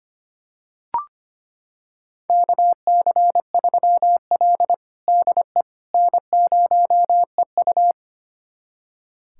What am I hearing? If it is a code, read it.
Morse code, heard as a keyed tone, about 25 wpm, 700 Hz. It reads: KC3L DE N0EU
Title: USA Callsign DE Callsign - 4 Characters - 25wpm